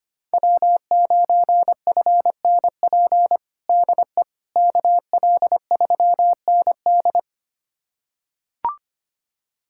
Morse code, heard as a keyed tone, about 25 wpm, 700 Hz. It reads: W9FNP DE KL3ND
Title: USA Callsign DE Callsign - 5 Characters - 25wpm